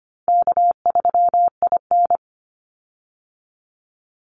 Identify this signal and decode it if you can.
Morse code, heard as a keyed tone, about 25 wpm, 700 Hz. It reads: K3IN
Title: Call Signs - 25wpm